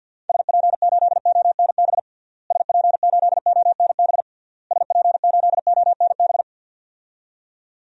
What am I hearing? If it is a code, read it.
Morse code, heard as a keyed tone, about 52 wpm, 700 Hz. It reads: SP8ONB SP8ONB SP8ONB
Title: DXCC-100 Call Signs - 6 Characters - Repeated 3x - 50wpm